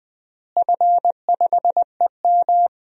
Morse code, heard as a keyed tone, about 20 wpm, 700 Hz. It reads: F5EM
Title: DXCC-100 Call Signs - 4 Characters - Repeated 3x - 20wpm